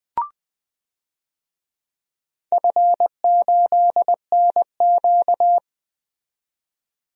Morse code, heard as a keyed tone, about 21 wpm, 700 Hz. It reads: F8NQ
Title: DXCC-100 Call Signs - 4 Characters - 20wpm